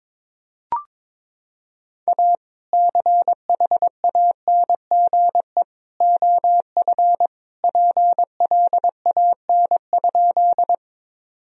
Morse code, heard as a keyed tone, about 22 wpm, 700 Hz. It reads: A CHANGE OF PLAN?
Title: Sentences from Top 1000 Words - Review Part 2 - 22wpm